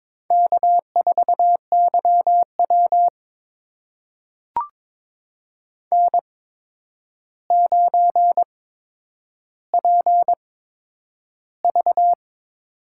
Morse code, heard as a keyed tone, about 22 wpm, spaced as for 9 wpm, 700 Hz. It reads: K4YW N9PV
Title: USA Call Signs - 4 Characters - 8x Character Spacing - 22wpm